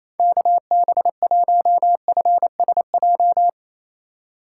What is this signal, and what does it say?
Morse code, heard as a keyed tone, about 28 wpm, 700 Hz. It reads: KB1FSJ
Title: USA Call Signs - 6 Characters - 2x Character Spacing - 28wpm